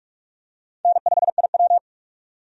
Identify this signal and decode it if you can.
Morse code, heard as a keyed tone, about 42 wpm, 700 Hz. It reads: N5IW
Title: USA Call Signs - 4 Characters - 45wpm